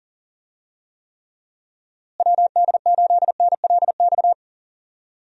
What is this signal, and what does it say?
Morse code, heard as a keyed tone, about 40 wpm, 700 Hz. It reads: WD8NLX
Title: USA Call Signs - 6 Characters - Repeated 2x - 40wpm